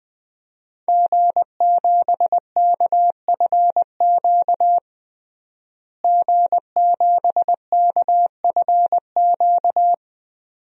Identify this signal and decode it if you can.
Morse code, heard as a keyed tone, about 20 wpm, 700 Hz. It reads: G7KFQ G7KFQ
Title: DXCC-100 Call Signs - 5 Characters - Repeated 2x - 20wpm